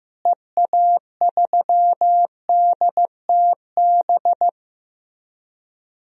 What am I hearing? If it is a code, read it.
Morse code, heard as a keyed tone, about 15 wpm, 700 Hz. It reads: EA3DTB
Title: DXCC-100 Call Signs - 6 Characters - Repeated 3x - 15wpm